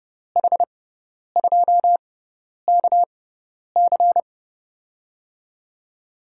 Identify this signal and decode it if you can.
Morse code, heard as a keyed tone, about 30 wpm, spaced as for 10 wpm, 700 Hz. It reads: H2KC
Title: USA Call Signs - 5 Characters - 6x Character Spacing - 30wpm